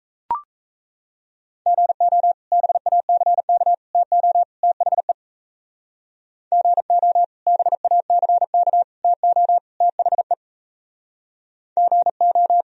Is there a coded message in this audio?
Yes: Morse code from 1.7 to 12.6 s, heard as a keyed tone, about 37 wpm, 690 Hz. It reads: GO BACK TO THE GO BACK TO THE GO